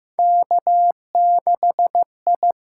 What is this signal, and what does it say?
Morse code, heard as a keyed tone, about 15 wpm, 700 Hz. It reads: K6I
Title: USA Call Signs - 6x Character Spacing - 15wpm